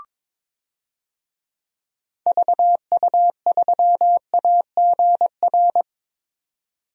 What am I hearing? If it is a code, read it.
Morse code, heard as a keyed tone, about 22 wpm, 700 Hz. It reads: VU3AGR
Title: DXCC-100 Call Signs - 6 Characters - 22wpm